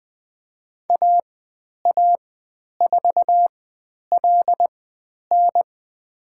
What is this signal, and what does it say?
Morse code, heard as a keyed tone, about 20 wpm, spaced as for 10 wpm, 700 Hz. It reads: AA4LN
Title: 102 Lesson 17g - Letter u - Call Signs 20wpm